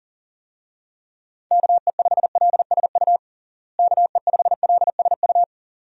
Morse code, heard as a keyed tone, about 40 wpm, 700 Hz. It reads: KE5LSU KE5LSU
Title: USA Call Signs - 6 Characters - Repeated 2x - 40wpm